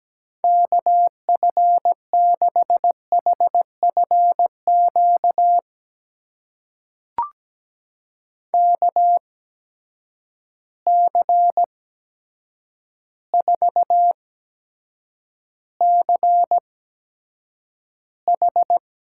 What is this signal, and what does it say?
Morse code, heard as a keyed tone, about 17 wpm, spaced as for 8 wpm, 700 Hz. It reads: KF6HFQ KC4CH
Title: USA Call Signs - 6 Characters - 8x Character Spacing - 17wpm